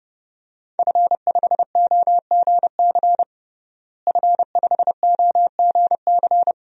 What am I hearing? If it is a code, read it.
Morse code, heard as a keyed tone, about 30 wpm, 700 Hz. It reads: F5OGC F5OGC
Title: DXCC-100 Call Signs - 5 Characters - Repeated 2x - 30wpm